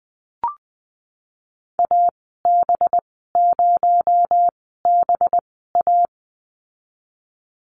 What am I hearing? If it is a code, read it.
Morse code, heard as a keyed tone, about 20 wpm, spaced as for 15 wpm, 700 Hz. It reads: AB0BA
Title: 238 Lesson 38g - Letter q - Call Signs 20wpm